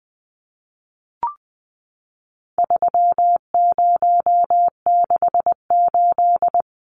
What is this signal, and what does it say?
Morse code, heard as a keyed tone, about 19 wpm, 700 Hz. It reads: 3068
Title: Numbers - 4-Digit 20wpm